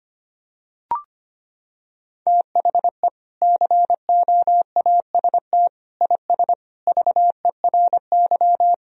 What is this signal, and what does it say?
Morse code, heard as a keyed tone, about 25 wpm, 700 Hz. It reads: THE COAST IS VERY
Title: Sentences from Top 1000 Words - Review Part 4 - 25wpm